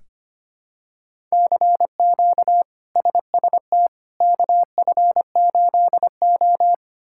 Morse code, heard as a keyed tone, about 25 wpm, 700 Hz. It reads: CQ SST KF8O
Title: K1USN SST QSO Practice - 25wpm